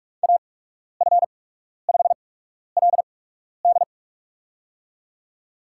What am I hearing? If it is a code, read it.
Morse code, heard as a keyed tone, about 46 wpm, spaced as for 12 wpm, 700 Hz. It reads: AF5LD
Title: USA Call Signs - 5 Characters - 8x Character Spacing - 45wpm